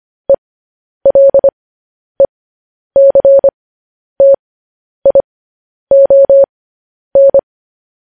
Morse code, heard as a keyed tone, about 25 wpm, spaced as for 10 wpm, 550 Hz. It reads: ELECTION
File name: podcast / News headlines-1x-25x10